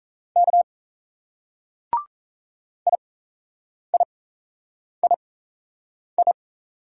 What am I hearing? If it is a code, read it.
Morse code, heard as a keyed tone, about 34 wpm, spaced as for 8 wpm, 700 Hz. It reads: K IIII
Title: Speed-Racing Single Letters 28wpm